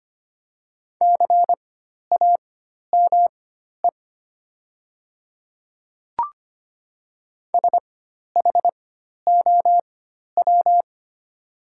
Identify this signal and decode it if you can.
Morse code, heard as a keyed tone, about 25 wpm, spaced as for 12 wpm, 700 Hz. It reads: CAME SHOW
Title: 4x Character Spacing - Top 200 Words - 25wpm